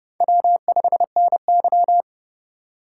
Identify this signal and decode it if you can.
Morse code, heard as a keyed tone, about 37 wpm, 700 Hz. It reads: W5NY
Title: Speed Racing - USA Call Signs 30wpm